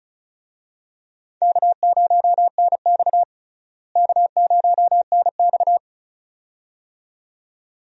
Morse code, heard as a keyed tone, about 36 wpm, 700 Hz. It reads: K0NX K0NX
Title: USA Call Signs - 4 Characters - Repeated 2x - 35wpm